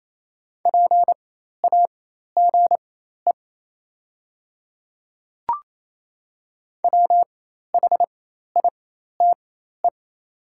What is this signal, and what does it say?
Morse code, heard as a keyed tone, about 28 wpm, spaced as for 13 wpm, 690 Hz. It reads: PAGE WHITE
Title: 4x Character Spacing - Top 300 Words - 28wpm